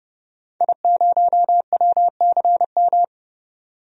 Morse code, heard as a keyed tone, about 30 wpm, 700 Hz. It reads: I0WCM
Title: DXCC-100 Call Signs - 5 Characters - 30wpm